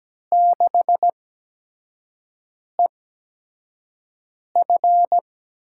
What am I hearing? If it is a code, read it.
Morse code, heard as a keyed tone, about 17 wpm, spaced as for 5 wpm, 700 Hz. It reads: 6EF
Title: USA Call Signs - 8x Character Spacing - 17wpm